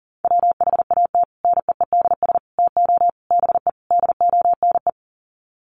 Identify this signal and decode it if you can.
Morse code, heard as a keyed tone, about 40 wpm, 700 Hz. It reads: WHAT NEEDS TO BE DONE
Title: Sets of 5 Words Encore 40wpm